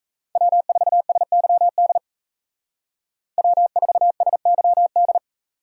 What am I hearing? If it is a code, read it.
Morse code, heard as a keyed tone, about 37 wpm, 700 Hz. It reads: W4SYD W4SYD